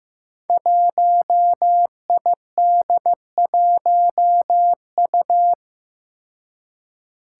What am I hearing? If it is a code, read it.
Morse code, heard as a keyed tone, about 15 wpm, 700 Hz. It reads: 1ID1U
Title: Five Challenging Letter-Number Combinations 15wpm